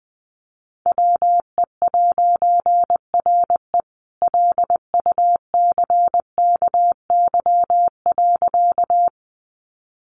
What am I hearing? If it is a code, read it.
Morse code, heard as a keyed tone, about 20 wpm, 700 Hz. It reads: WE'RE LUCKY.
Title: A Sound Of Thunder 20wpm